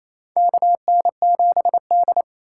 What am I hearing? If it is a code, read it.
Morse code, heard as a keyed tone, about 28 wpm, 700 Hz. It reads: KN7D
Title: K1USN SST QSO Practice - 28wpm